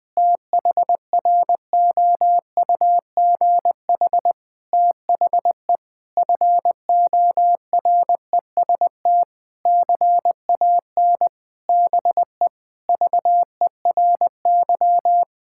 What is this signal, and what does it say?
Morse code, heard as a keyed tone, about 20 wpm, 700 Hz. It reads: THROUGH THE FOREST CAN BE VERY
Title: Difficult Sentences with Hints - 20wpm